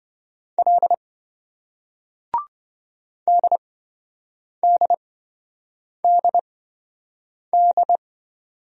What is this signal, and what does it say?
Morse code, heard as a keyed tone, about 27 wpm, spaced as for 7 wpm, 700 Hz. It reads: L DDDD